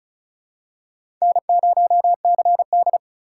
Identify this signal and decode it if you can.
Morse code, heard as a keyed tone, about 35 wpm, 700 Hz. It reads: N0CD
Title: USA Call Signs - 4 Characters - 4x Character Spacing - 35wpm